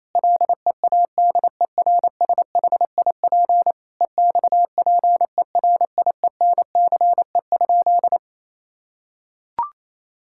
Morse code, heard as a keyed tone, about 28 wpm, 700 Hz. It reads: LEADERSHIP EXPERIENCE?
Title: Sentences from Top 1600 Words 28wpm